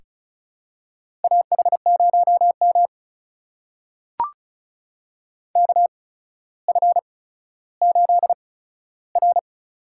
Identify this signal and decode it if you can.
Morse code, heard as a keyed tone, about 36 wpm, spaced as for 15 wpm, 700 Hz. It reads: AH0M KF8R
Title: USA Call Signs - 4 Characters - 8x Character Spacing - 35wpm